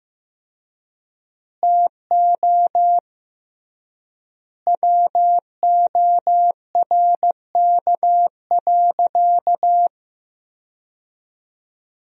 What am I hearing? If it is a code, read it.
Morse code, heard as a keyed tone, about 15 wpm, 700 Hz. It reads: TO WORK.
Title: 3x Word-Spacing - Sentences from Top 300 Words - 15wpm